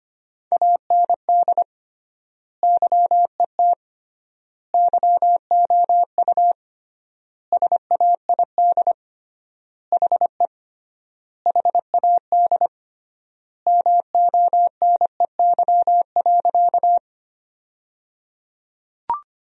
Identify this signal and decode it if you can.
Morse code, heard as a keyed tone, about 25 wpm, 700 Hz. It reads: AND YET YOU SAID HE HAD MONEY.
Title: 3x Word-Spacing - Sentences from Top 500 Words - 25wpm